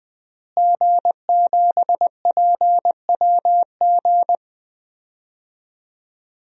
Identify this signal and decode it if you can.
Morse code, heard as a keyed tone, about 21 wpm, 700 Hz. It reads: G7PWG
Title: DXCC-100 Call Signs - 5 Characters - Repeated 3x - 20wpm